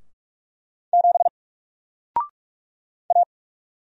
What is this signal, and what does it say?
Morse code, heard as a keyed tone, about 48 wpm, 700 Hz. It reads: 7 A
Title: Single Letter-Number Alt 45wpm